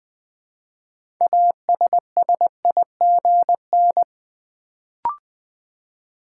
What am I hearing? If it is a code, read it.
Morse code, heard as a keyed tone, about 20 wpm, 700 Hz. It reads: ASSIGN
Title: Build A Word - 3 Letters At A Time - 6-Letter Words - 20wpm